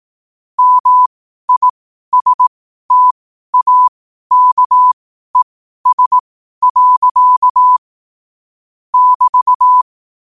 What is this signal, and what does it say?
Morse code, heard as a keyed tone, about 18 wpm, spaced as for 12 wpm, 1000 Hz. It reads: MISTAKES. <BT>
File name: qotdfiles / 12wpm